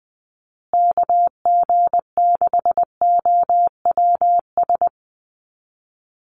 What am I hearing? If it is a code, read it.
Morse code, heard as a keyed tone, about 20 wpm, 700 Hz. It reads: KG6OWS
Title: USA Call Signs Repeated 3x 20wpm